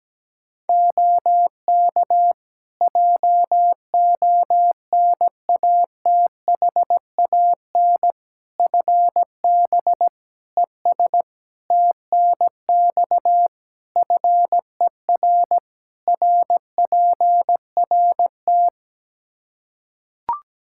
Morse code, heard as a keyed tone, about 17 wpm, 700 Hz. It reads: OK JONATHAN FB ES TNX FER RPRT
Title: Practice QSO - Exchange 2 - Info - 17wpm